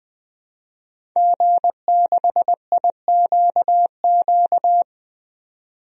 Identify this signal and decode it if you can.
Morse code, heard as a keyed tone, about 20 wpm, 700 Hz. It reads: G6IQQ
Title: DXCC-100 Call Signs - 5 Characters - 20wpm